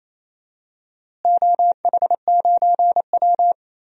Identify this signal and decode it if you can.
Morse code, heard as a keyed tone, about 27 wpm, 700 Hz. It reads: OH9W
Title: DXCC-100 Call Signs - 4 Characters - Repeated 2x - 28wpm